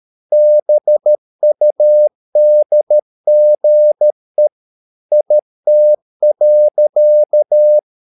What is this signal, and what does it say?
Morse code, heard as a keyed tone, about 13 wpm, 600 Hz. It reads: BUDGE IT.
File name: mp3 files / Lexiphiles - 13wpm